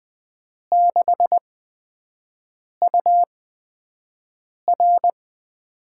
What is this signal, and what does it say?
Morse code, heard as a keyed tone, about 20 wpm, spaced as for 5 wpm, 700 Hz. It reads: 6UR
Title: USA Call Signs - 6 Characters - 8x Character Spacing - 20wpm